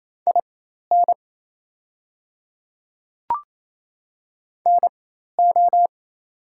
Morse code, heard as a keyed tone, about 27 wpm, spaced as for 13 wpm, 700 Hz. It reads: IN NO
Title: 4x Character Spacing - Top 100 Words - 28wpm